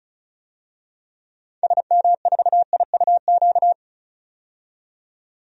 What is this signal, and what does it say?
Morse code, heard as a keyed tone, about 36 wpm, 700 Hz. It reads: SM4IUQ
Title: DXCC-100 Call Signs - 6 Characters - 35wpm